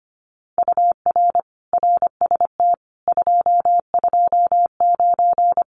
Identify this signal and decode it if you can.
Morse code, heard as a keyed tone, about 25 wpm, 700 Hz. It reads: UR RST 229
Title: Common QSO Phrases Encore 25wpm